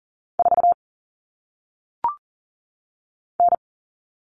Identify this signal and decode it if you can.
Morse code, heard as a keyed tone, about 40 wpm, 700 Hz. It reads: 4 N
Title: All Cut Numbers - 1-Digit 40wpm